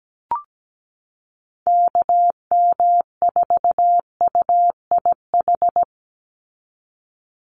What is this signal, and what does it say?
Morse code, heard as a keyed tone, about 17 wpm, 700 Hz. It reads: KM4UIH
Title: USA Call Signs 17wpm